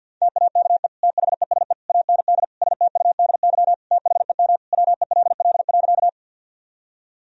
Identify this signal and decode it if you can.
Morse code, heard as a keyed tone, about 51 wpm, 700 Hz. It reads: TAKE THESE AND STUDY THEM WELL.
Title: Sentences from Top 1000 Words - Review Part 1 - 50wpm